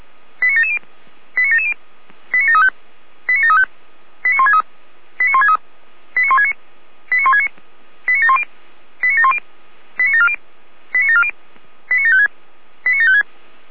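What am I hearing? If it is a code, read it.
Five-tone selective call (ZVEI), heard as quick runs of five tones, tones of about 80 ms. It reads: 89770 89770 89735 89735 89163 89163 89169 89169 89710 89710 89740 89740 89756 89756
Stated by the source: ZVEI five-tone selective call received by SDR.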